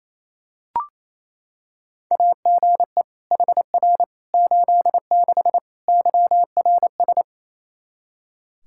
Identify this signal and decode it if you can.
Morse code, heard as a keyed tone, about 28 wpm, 700 Hz. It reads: AGE HR 86 YRS
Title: Practice QSO - Exchange 3 - More Info - 28wpm